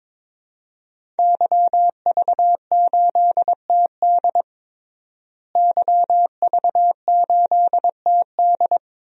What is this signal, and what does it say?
Morse code, heard as a keyed tone, about 22 wpm, 700 Hz. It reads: YV8TD YV8TD
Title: DXCC-100 Call Signs - 5 Characters - Repeated 2x - 22wpm